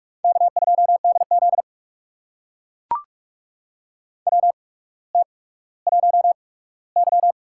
Morse code, heard as a keyed tone, about 44 wpm, spaced as for 19 wpm, 700 Hz. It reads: K2DZ WT1Y
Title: USA Call Signs - 8x Character Spacing - 45wpm